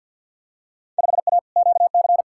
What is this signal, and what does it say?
Morse code, heard as a keyed tone, about 51 wpm, 700 Hz. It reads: 5A QC
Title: Field Day Exchanges Encore - 50wpm